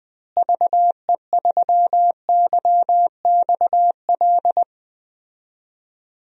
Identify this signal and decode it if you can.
Morse code, heard as a keyed tone, about 20 wpm, 700 Hz. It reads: VE3YXL
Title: DXCC-100 Call Signs - 6 Characters - Repeated 2x - 20wpm